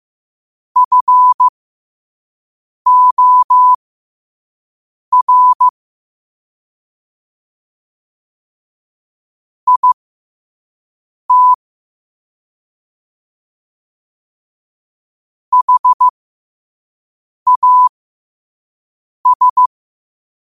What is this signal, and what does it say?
Morse code, heard as a keyed tone, about 15 wpm, spaced as for 5 wpm, 1000 Hz. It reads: FOR IT HAS
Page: New quotes every day in morse code at 5 Words per minute.